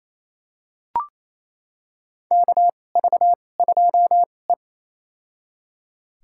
Morse code, heard as a keyed tone, about 28 wpm, spaced as for 20 wpm, 700 Hz. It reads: KV2E
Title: USA Call Signs - 4 Characters - 2x Character Spacing - 28wpm